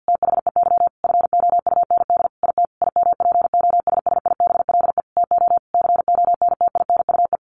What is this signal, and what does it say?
Morse code, heard as a keyed tone, about 50 wpm, 700 Hz. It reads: THEY FOUND IT IMPOSSIBLE TO CONTINUE
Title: Sentences from Top 1100 Words 50wpm